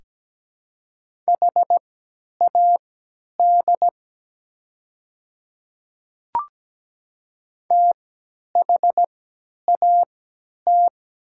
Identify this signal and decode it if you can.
Morse code, heard as a keyed tone, about 17 wpm, spaced as for 10 wpm, 700 Hz. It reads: HAD THAT
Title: 3x Character Spacing - Top 200 Words - 17wpm